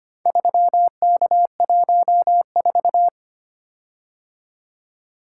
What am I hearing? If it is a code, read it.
Morse code, heard as a keyed tone, about 25 wpm, 700 Hz. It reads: 3K14
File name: practice / 25wpm / Four Letter-Number Combinations - 25wpm